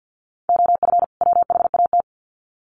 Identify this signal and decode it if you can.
Morse code, heard as a keyed tone, about 50 wpm, 700 Hz. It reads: OF WHAT
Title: Sets of 2 Words 50wpm